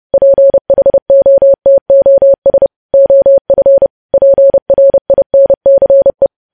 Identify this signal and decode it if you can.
Morse code, heard as a keyed tone, about 30 wpm, 550 Hz. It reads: PHOTOS OF PRINCE
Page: news headlines in morse code at 30 wpm